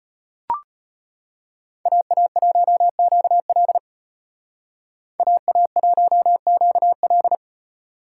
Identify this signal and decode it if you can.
Morse code, heard as a keyed tone, about 32 wpm, 700 Hz. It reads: AA1QL AA1QL